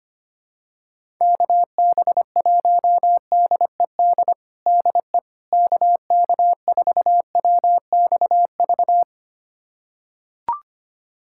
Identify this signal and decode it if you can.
Morse code, heard as a keyed tone, about 25 wpm, 700 Hz. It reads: KB1DED DE KK4WXV
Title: USA Callsign DE Callsign - 6 Characters - 25wpm